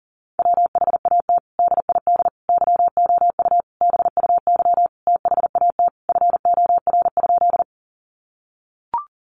Morse code, heard as a keyed tone, about 40 wpm, 700 Hz. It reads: WHAT DID YOU BUY THAT FOR?
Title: Sentences from Top 700 Words 40wpm